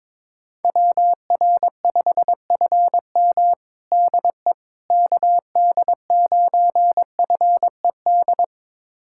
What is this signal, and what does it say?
Morse code, heard as a keyed tone, about 22 wpm, 700 Hz. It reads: WR5FM DE KD9FED
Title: USA Callsign DE Callsign 22wpm